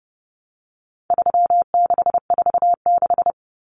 Morse code, heard as a keyed tone, about 30 wpm, 700 Hz. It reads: 3646
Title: Common Cut Numbers - 4-Digit 30wpm